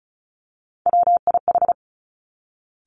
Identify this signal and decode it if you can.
Morse code, heard as a keyed tone, about 36 wpm, 700 Hz. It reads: WIH
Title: Three Challenging Letter-Number Combinations 35wpm